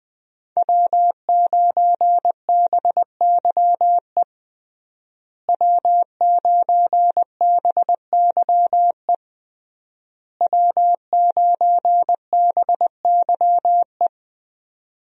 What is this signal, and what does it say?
Morse code, heard as a keyed tone, about 20 wpm, 700 Hz. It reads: W9BYE W9BYE W9BYE
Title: USA Call Signs - 5 Characters - Repeated 3x - 20wpm